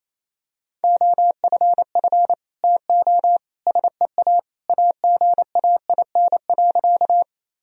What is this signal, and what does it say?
Morse code, heard as a keyed tone, about 28 wpm, 700 Hz. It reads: OFF TO SEA AGAIN.
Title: Sentences from Top 1000 Words - Review Part 1 - 28wpm